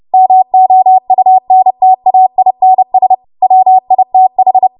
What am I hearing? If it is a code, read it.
Morse code, heard as a keyed tone, about 30 wpm, 750 Hz. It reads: MOUNTAINS WITH
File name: Morse / 30WPM